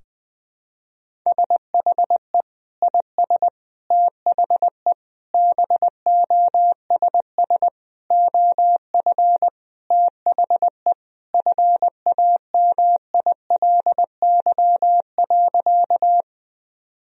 Morse code, heard as a keyed tone, about 20 wpm, 700 Hz. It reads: SHE IS THE BOSS OF THE FAMILY.
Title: Sentences from Top 1000 Words - Review Part 4 - 20wpm